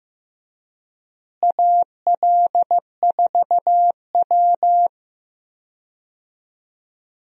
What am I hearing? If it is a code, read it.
Morse code, heard as a keyed tone, about 15 wpm, 700 Hz. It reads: AL4W
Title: USA Call Signs - 4 Characters - Repeated 2x - 15wpm